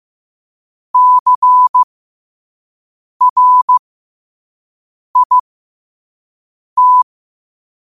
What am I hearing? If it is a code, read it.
Morse code, heard as a keyed tone, about 15 wpm, spaced as for 5 wpm, 1000 Hz. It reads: CRIT
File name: qotdfiles / 5wpm